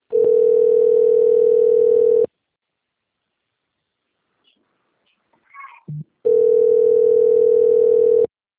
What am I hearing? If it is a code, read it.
Ringback tone (North America), 440 + 480 Hz.